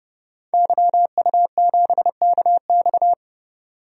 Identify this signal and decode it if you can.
Morse code, heard as a keyed tone, about 32 wpm, 700 Hz. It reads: YU7KX
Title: DXCC-100 Call Signs - 5 Characters - 30wpm